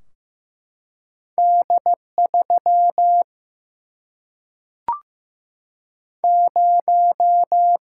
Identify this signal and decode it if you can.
Morse code, heard as a keyed tone, about 15 wpm, 700 Hz. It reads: D 3 0
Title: Single-Letter Single-Number 15wpm